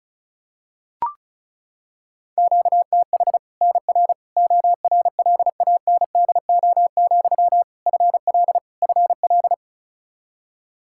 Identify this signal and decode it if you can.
Morse code, heard as a keyed tone, about 36 wpm, 700 Hz. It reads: QTH NR ORLANDO, FL FL
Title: Practice QSO - Exchange 1 - Beginning - 35wpm